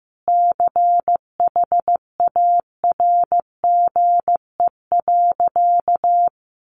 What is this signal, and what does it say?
Morse code, heard as a keyed tone, about 15 wpm, 700 Hz. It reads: CHARGE.
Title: Sentences from Top 900 Words 15wpm